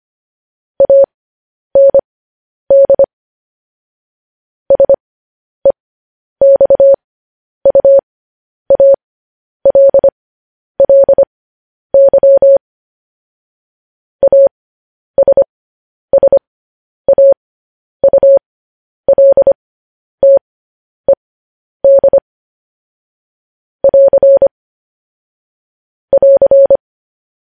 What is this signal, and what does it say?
Morse code, heard as a keyed tone, about 25 wpm, spaced as for 10 wpm, 550 Hz. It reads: AND SEXUALLY ASSAULTED <AR> <AR>